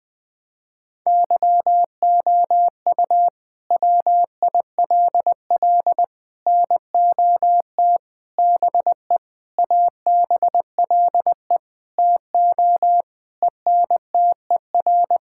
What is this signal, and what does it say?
Morse code, heard as a keyed tone, about 20 wpm, 700 Hz. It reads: YOU WILL NOT BE ABLE TO ENTER
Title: Sentences with 6-Word Phrases - 20wpm